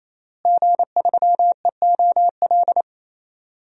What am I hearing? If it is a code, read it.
Morse code, heard as a keyed tone, about 27 wpm, 700 Hz. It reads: G3EOL
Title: DXCC-100 Call Signs - 5 Characters - Repeated 3x - 28wpm